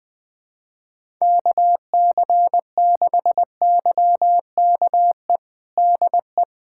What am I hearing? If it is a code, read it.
Morse code, heard as a keyed tone, about 20 wpm, 700 Hz. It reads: KC6YKE DE
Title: USA Callsign DE Callsign 20wpm